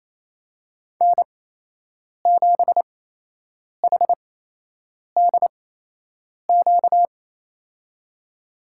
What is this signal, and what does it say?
Morse code, heard as a keyed tone, about 28 wpm, spaced as for 8 wpm, 700 Hz. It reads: N7HDQ
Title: USA Call Signs - 5 Characters - 8x Character Spacing - 28wpm